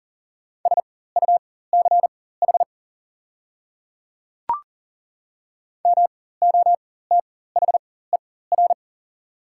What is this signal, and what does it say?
Morse code, heard as a keyed tone, about 40 wpm, spaced as for 19 wpm, 690 Hz. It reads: SUCH MOTHER
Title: 4x Character Spacing - Top 500 Words - 40wpm